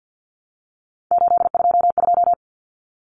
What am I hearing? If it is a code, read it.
Morse code, heard as a keyed tone, about 49 wpm, 700 Hz. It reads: 822
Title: Numbers - 3-Digit 50wpm